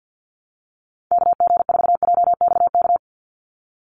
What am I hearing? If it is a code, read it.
Morse code, heard as a keyed tone, about 49 wpm, 700 Hz. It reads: KG4JXK
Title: USA Call Signs 50wpm